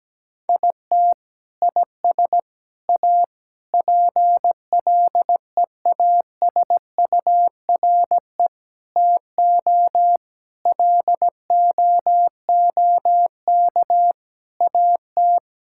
Morse code, heard as a keyed tone, about 17 wpm, 700 Hz. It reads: IT IS A PLEASURE TO LOOK AT
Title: Sentences from Top 1600 Words 17wpm